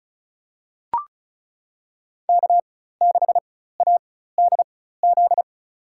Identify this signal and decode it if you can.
Morse code, heard as a keyed tone, about 34 wpm, spaced as for 16 wpm, 700 Hz. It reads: K6ADZ
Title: USA Call Signs - 5 Characters - 4x Character Spacing - 35wpm